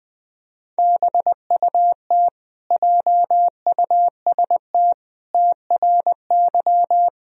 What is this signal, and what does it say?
Morse code, heard as a keyed tone, about 20 wpm, 700 Hz. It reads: BUT JUST TRY
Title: Sentences from Top 1000 Words - Review Part 1 - 20wpm